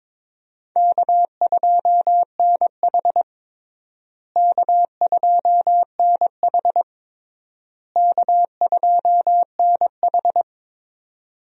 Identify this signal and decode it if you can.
Morse code, heard as a keyed tone, about 22 wpm, 700 Hz. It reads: K2NH K2NH K2NH
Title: USA Call Signs - 4 Characters - Repeated 3x - 22wpm